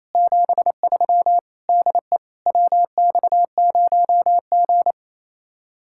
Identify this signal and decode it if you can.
Morse code, heard as a keyed tone, about 32 wpm, 700 Hz. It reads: 73 DE WX0G
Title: Speed-Racing Common QSO Phrases - 28wpm